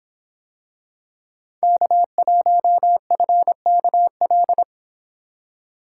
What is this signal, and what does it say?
Morse code, heard as a keyed tone, about 25 wpm, 700 Hz. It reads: K1FKL